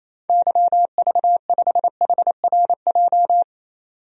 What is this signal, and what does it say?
Morse code, heard as a keyed tone, about 28 wpm, 700 Hz. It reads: YV5HRJ
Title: DXCC-100 Call Signs - 6 Characters - 28wpm